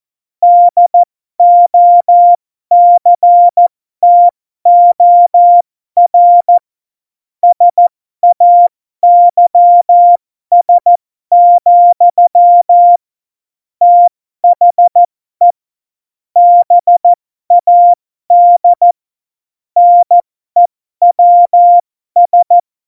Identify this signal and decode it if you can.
Morse code, heard as a keyed tone, about 14 wpm, spaced as for 12 wpm, 700 Hz. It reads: DOCTOR SAYS, THE BAD NEWS